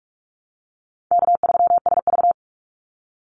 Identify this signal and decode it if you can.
Morse code, heard as a keyed tone, about 44 wpm, 700 Hz. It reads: K3SV
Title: Easy Call Signs Repeated 2x - 45wpm